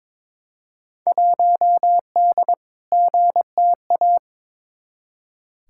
Morse code, heard as a keyed tone, about 22 wpm, 700 Hz. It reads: 1D GTA
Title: Field Day Exchanges - 22wpm